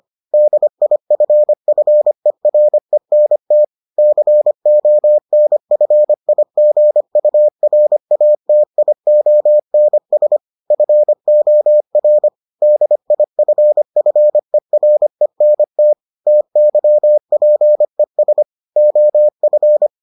Morse code, heard as a keyed tone, about 25 wpm, 600 Hz. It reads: DIFFERENT CONFIGURATIONS FOR DIFFERENT TYPES OF